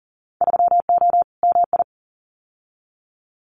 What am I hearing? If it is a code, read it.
Morse code, heard as a keyed tone, about 40 wpm, 700 Hz. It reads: 3O MI
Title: Winter Field Day Exchanges 40wpm